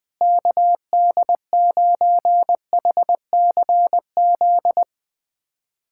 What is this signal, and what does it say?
Morse code, heard as a keyed tone, about 20 wpm, 700 Hz. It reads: KD9HCZ
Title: USA Callsign DE Callsign - 6 Characters - 20wpm